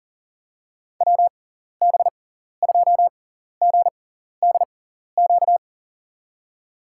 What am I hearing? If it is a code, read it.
Morse code, heard as a keyed tone, about 39 wpm, spaced as for 14 wpm, 700 Hz. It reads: WB2GDQ
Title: USA Call Signs - 6 Characters - 6x Character Spacing - 40wpm